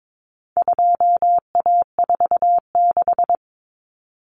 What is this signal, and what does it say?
Morse code, heard as a keyed tone, about 22 wpm, 700 Hz. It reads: 2A46
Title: Common Cut Numbers - 4-Digit 22wpm